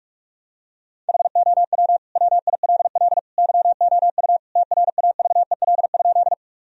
Morse code, heard as a keyed tone, about 46 wpm, 700 Hz. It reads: HOW WILL YOU TRAVEL?
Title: Sentences from Top 1000 Words - Review Part 2 - 45wpm